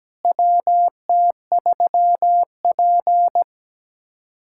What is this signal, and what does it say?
Morse code, heard as a keyed tone, about 17 wpm, 700 Hz. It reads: WT3P
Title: USA Callsign DE Callsign - 4 Characters - 17wpm